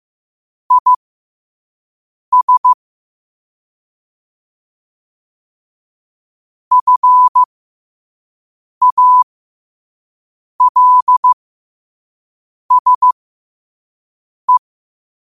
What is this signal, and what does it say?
Morse code, heard as a keyed tone, about 15 wpm, spaced as for 5 wpm, 1000 Hz. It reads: IS FALSE
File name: qotdfiles / 5wpm